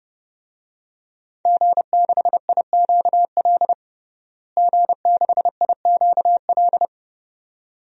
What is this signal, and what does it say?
Morse code, heard as a keyed tone, about 30 wpm, 700 Hz. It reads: G6IQL G6IQL
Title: DXCC-100 Call Signs - 5 Characters - Repeated 2x - 30wpm